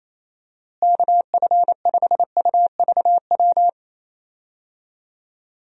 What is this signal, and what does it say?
Morse code, heard as a keyed tone, about 28 wpm, 700 Hz. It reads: KF5UVW
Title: USA Call Signs - 6 Characters - Repeated 2x - 28wpm